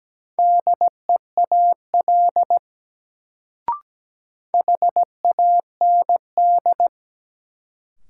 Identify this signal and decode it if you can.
Morse code, heard as a keyed tone, about 17 wpm, 700 Hz. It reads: DEAL HAND
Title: Sets of 3 Words - New Word Drill - 17wpm